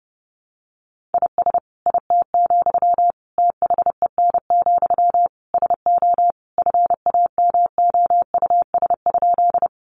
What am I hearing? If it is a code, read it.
Morse code, heard as a keyed tone, about 30 wpm, 700 Hz. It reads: IS IT, THEN, SO FAMOUS?
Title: Sentences from Top 1500 Words 30wpm